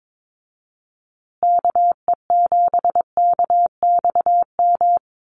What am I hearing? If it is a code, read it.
Morse code, heard as a keyed tone, about 22 wpm, 700 Hz. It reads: KE7KXM
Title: USA Call Signs Repeated 2x 22wpm